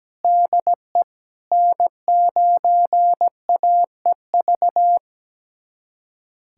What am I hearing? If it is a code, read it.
Morse code, heard as a keyed tone, about 17 wpm, 700 Hz. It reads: DE N9AEV
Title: USA Callsign DE Callsign - 5 Characters - 17wpm